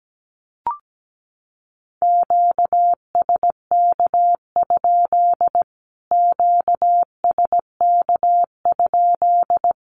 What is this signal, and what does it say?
Morse code, heard as a keyed tone, about 17 wpm, 700 Hz. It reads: QSK? QSK?
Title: Q-Codes Repeated 3x 17wpm